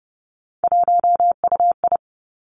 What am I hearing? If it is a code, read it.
Morse code, heard as a keyed tone, about 30 wpm, 700 Hz. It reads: 1UI
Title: Three Challenging Letter-Number Combinations 30wpm